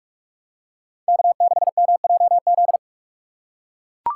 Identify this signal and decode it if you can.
Morse code, heard as a keyed tone, about 41 wpm, 700 Hz. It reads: K6MJZ